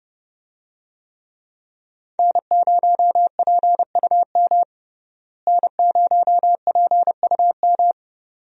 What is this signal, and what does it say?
Morse code, heard as a keyed tone, about 30 wpm, 700 Hz. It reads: N0PUM N0PUM
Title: USA Call Signs - 5 Characters - Repeated 2x - 30wpm